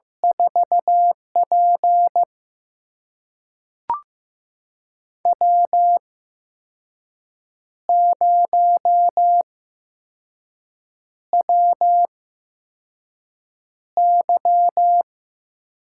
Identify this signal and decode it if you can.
Morse code, heard as a keyed tone, about 15 wpm, spaced as for 5 wpm, 700 Hz. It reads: 4P W0WY
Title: USA Call Signs - 4 Characters - 8x Character Spacing - 15wpm